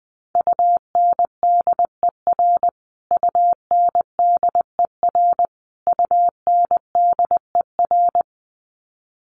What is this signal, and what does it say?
Morse code, heard as a keyed tone, about 20 wpm, 700 Hz. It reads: UNDER UNDER UNDER
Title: Top 200 Words Repeated 20wpm